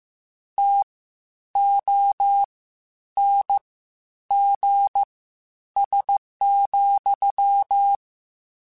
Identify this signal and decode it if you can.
Morse code, heard as a keyed tone, about 15 wpm, spaced as for 9 wpm, 800 Hz. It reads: TONGS,
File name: Misc / Misc-mp3s / WordPairs sp @13